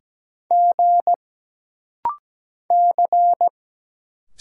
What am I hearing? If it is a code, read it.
Morse code, heard as a keyed tone, about 17 wpm, 700 Hz. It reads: G C
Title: Single Letters - Rapid-Fire 17wpm